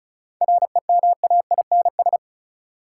Morse code, heard as a keyed tone, about 36 wpm, 700 Hz. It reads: REMAINS
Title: Sets of 5 Words Encore - New Word Drill - 35wpm